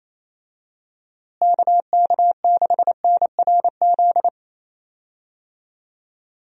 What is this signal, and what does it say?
Morse code, heard as a keyed tone, about 27 wpm, 700 Hz. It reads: KK6NRZ
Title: USA Call Signs - 6 Characters - Repeated 2x - 28wpm